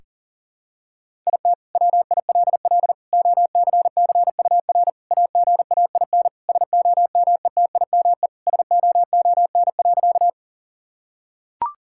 Morse code, heard as a keyed tone, about 40 wpm, 700 Hz. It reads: IT WILL OCCUR AGAIN SOMETIME SOON.
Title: Sentences from Top 1000 Words - Review Part 6 - 40wpm